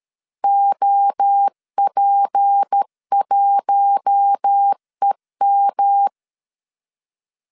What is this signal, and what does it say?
Morse code, heard as a keyed tone, about 13 wpm, 800 Hz. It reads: OP1EM
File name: LEZIONI / 6 PKX0 / 6 P 12wpm 2033